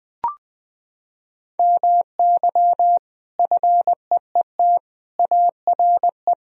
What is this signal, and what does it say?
Morse code, heard as a keyed tone, about 20 wpm, 700 Hz. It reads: MY FEET ARE
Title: Sentences from Top 1000 Words - Review Part 1 - 20wpm